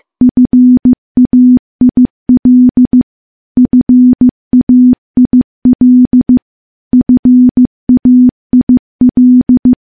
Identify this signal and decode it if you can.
Morse code, heard as a keyed tone, about 15 wpm, 250 Hz. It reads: FAIL FAIL FAIL
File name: morse / morse-training-15WPM